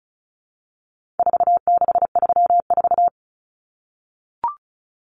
Morse code, heard as a keyed tone, about 34 wpm, 700 Hz. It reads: 4634
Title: Numbers - 4-Digit 35wpm